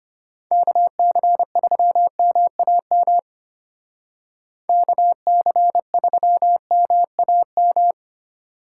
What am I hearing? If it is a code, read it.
Morse code, heard as a keyed tone, about 30 wpm, 700 Hz. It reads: KC3MAM KC3MAM
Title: Speed Racing - USA 2x3 Call Signs 25wpm